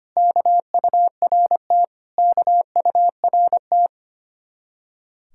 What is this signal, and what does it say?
Morse code, heard as a keyed tone, about 33 wpm, 700 Hz. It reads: KURT KURT